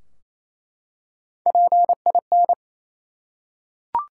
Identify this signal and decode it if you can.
Morse code, heard as a keyed tone, about 27 wpm, 700 Hz. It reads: PIN
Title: Three Letter Words - 28wpm